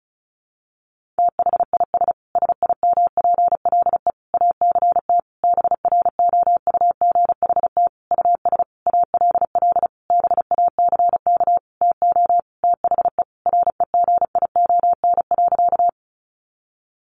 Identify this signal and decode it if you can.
Morse code, heard as a keyed tone, about 34 wpm, 700 Hz. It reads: THIS SIMPLE ACT BROUGHT US ALL BACK TO THE REGION.
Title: Sentences from Top 1500 Words 35wpm